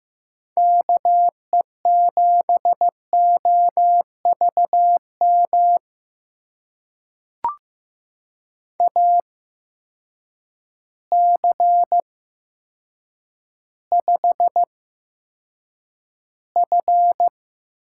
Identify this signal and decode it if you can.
Morse code, heard as a keyed tone, about 15 wpm, spaced as for 8 wpm, 700 Hz. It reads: KE7OVM AC5F
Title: USA Call Signs - 8x Character Spacing - 15wpm